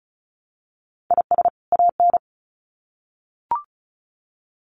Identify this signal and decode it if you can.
Morse code, heard as a keyed tone, about 34 wpm, 700 Hz. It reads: IS AN
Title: Sets of 2 Words 35wpm